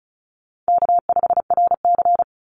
Morse code, heard as a keyed tone, about 34 wpm, 700 Hz. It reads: K5RC
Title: Call Signs Repeated 2x - 35wpm